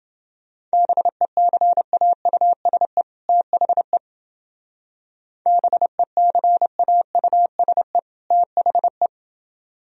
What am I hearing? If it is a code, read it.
Morse code, heard as a keyed tone, about 29 wpm, 700 Hz. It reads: BECAUSE THE BECAUSE THE